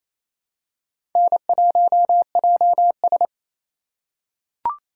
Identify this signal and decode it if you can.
Morse code, heard as a keyed tone, about 28 wpm, 700 Hz. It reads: N1JS
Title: USA Call Signs - 4 Characters - 4x Character Spacing - 28wpm